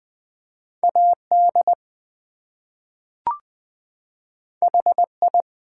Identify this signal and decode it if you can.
Morse code, heard as a keyed tone, about 20 wpm, 700 Hz. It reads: AD HI
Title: Two Letter Words - 20wpm